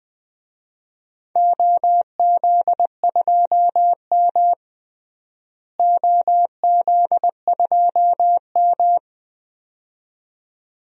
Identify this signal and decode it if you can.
Morse code, heard as a keyed tone, about 20 wpm, 700 Hz. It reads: OZ2M OZ2M
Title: DXCC-100 Call Signs - 4 Characters - Repeated 2x - 20wpm